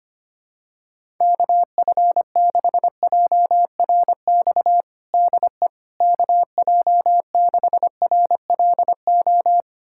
Morse code, heard as a keyed tone, about 25 wpm, 700 Hz. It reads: KF6JRX DE KJ6RLO
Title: USA Callsign DE Callsign 25wpm